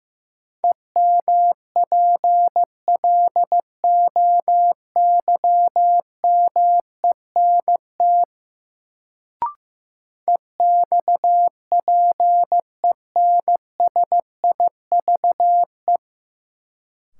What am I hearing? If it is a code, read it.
Morse code, heard as a keyed tone, about 15 wpm, 700 Hz. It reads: EMPLOYMENT EXPENSIVE
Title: Sentences from Top 1500 Words - New Word Drill Alt - 15wpm